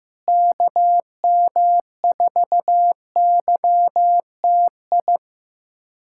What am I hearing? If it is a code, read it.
Morse code, heard as a keyed tone, about 16 wpm, 700 Hz. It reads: KM4YTI
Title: USA Callsign DE Callsign - 6 Characters - 15wpm